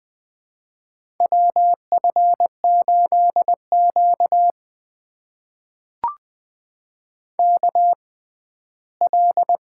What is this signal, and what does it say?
Morse code, heard as a keyed tone, about 20 wpm, spaced as for 14 wpm, 700 Hz. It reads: WF8Q KL
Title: USA Call Signs - 4 Characters - 6x Character Spacing - 20wpm